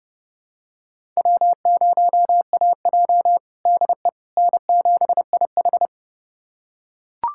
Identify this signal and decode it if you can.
Morse code, heard as a keyed tone, about 30 wpm, 700 Hz. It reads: W0AJ DE N7IH
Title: USA Callsign DE Callsign - 4 Characters - 30wpm